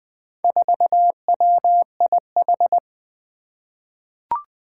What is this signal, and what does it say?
Morse code, heard as a keyed tone, about 20 wpm, 700 Hz. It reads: 4WIH
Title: Four Challenging Letter-Number Combinations 20wpm